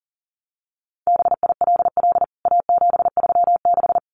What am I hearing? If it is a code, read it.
Morse code, heard as a keyed tone, about 40 wpm, 700 Hz. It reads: BILL A736
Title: CWT Exchange 40wpm